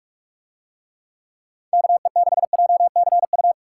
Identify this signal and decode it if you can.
Morse code, heard as a keyed tone, about 46 wpm, 700 Hz. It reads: KE6JCU